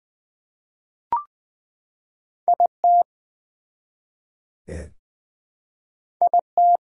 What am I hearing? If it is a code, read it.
Morse code, heard as a keyed tone, about 21 wpm, 700 Hz. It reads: IT IT
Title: Build A Phrase - 2-Word Phrases - 20wpm